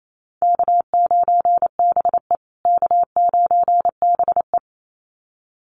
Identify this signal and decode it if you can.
Morse code, heard as a keyed tone, about 28 wpm, 700 Hz. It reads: K9BE K9BE
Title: USA Call Signs Repeated 2x 28wpm